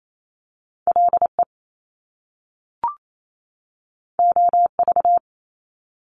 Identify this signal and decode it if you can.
Morse code, heard as a keyed tone, about 28 wpm, 700 Hz. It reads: LE OV
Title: Top Two Letter Combinations in Words 28wpm